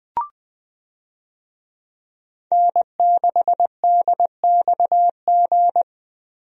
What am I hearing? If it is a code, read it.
Morse code, heard as a keyed tone, about 20 wpm, 700 Hz. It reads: N6DXG
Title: USA Call Signs - 5 Characters - Repeated 3x - 20wpm